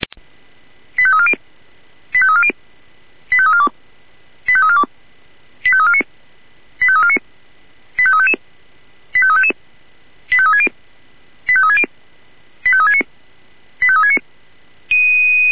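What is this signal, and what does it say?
Five-tone selective call (ZVEI), heard as quick runs of five tones, tones of about 70 ms. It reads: 85340 85340 85342 85342 85349 85349 85355 85360 85360 85369 85369
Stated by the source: ZVEI five-tone selective call received on SDR.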